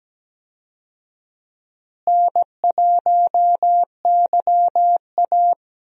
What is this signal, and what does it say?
Morse code, heard as a keyed tone, about 17 wpm, 700 Hz. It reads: N1YA
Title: USA Call Signs - 4 Characters - Repeated 3x - 17wpm